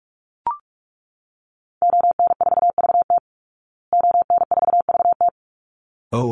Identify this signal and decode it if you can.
Morse code, heard as a keyed tone, about 44 wpm, 700 Hz. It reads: ON4VT ON4VT
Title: Call Signs Repeated 3x - 45wpm